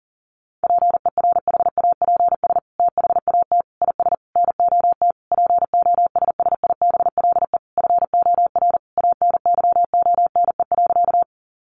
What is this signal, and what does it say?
Morse code, heard as a keyed tone, about 40 wpm, 700 Hz. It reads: PERHAPS THAT IS NOT POSSIBLE FOR ANYONE.
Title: Sentences from Top 900 Words 40wpm